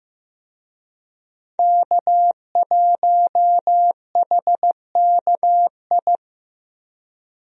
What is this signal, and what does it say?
Morse code, heard as a keyed tone, about 15 wpm, 700 Hz. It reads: K1HKI
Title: USA Call Signs - 5 Characters - Repeated 2x - 15wpm